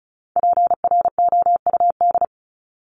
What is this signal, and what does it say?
Morse code, heard as a keyed tone, about 34 wpm, 700 Hz. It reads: PROUD
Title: Top 1501-1750 Words 35wpm